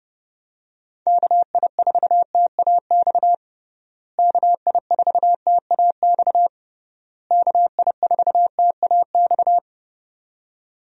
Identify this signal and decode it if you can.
Morse code, heard as a keyed tone, about 31 wpm, 700 Hz. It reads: KI4TAX KI4TAX KI4TAX
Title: USA Call Signs - 6 Characters - Repeated 3x - 30wpm